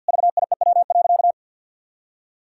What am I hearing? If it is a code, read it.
Morse code, heard as a keyed tone, about 50 wpm, 700 Hz. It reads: VIEW.
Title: Sentences from Top 1000 Words - Review Part 6 - 50wpm